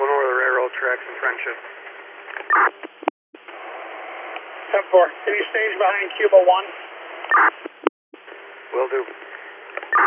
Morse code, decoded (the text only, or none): none